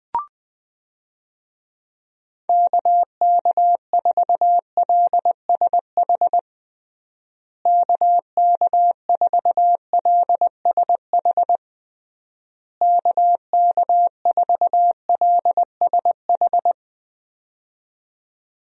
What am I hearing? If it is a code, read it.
Morse code, heard as a keyed tone, about 21 wpm, 700 Hz. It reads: KK4LSH KK4LSH KK4LSH
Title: USA Call Signs - 6 Characters - Repeated 3x - 20wpm